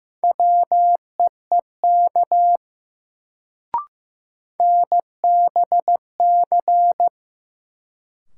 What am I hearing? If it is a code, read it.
Morse code, heard as a keyed tone, about 15 wpm, 700 Hz. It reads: WEEK NBC
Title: Sets of 5 Words Encore - New Word Drill - 15wpm